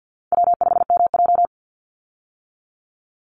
Morse code, heard as a keyed tone, about 50 wpm, 700 Hz. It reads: W5MJ
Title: Easy Call Signs - 50wpm